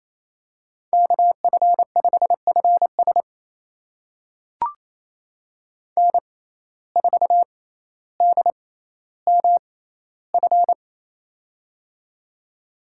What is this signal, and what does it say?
Morse code, heard as a keyed tone, about 28 wpm, spaced as for 14 wpm, 700 Hz. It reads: KF5FS N4DMF
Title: USA Call Signs - 5 Characters - 6x Character Spacing - 28wpm